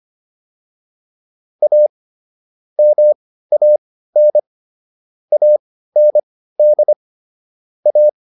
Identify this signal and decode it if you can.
Morse code, heard as a keyed tone, about 25 wpm, spaced as for 15 wpm, 600 Hz. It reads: A MAN AND A